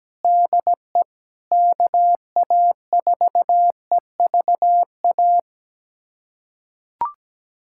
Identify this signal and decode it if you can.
Morse code, heard as a keyed tone, about 17 wpm, 700 Hz. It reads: DE KA4EVA
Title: USA Callsign DE Callsign - 6 Characters - 17wpm